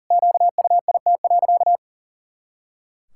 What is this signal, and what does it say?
Morse code, heard as a keyed tone, about 40 wpm, 700 Hz. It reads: QUIT.
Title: Pangram Challenge v2 - 40wpm